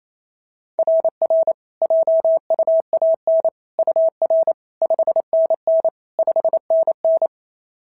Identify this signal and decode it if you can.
Morse code, heard as a keyed tone, about 25 wpm, 650 Hz. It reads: RR JUAN UR 5NN 5NN
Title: SOTA QSO Practice 28wpm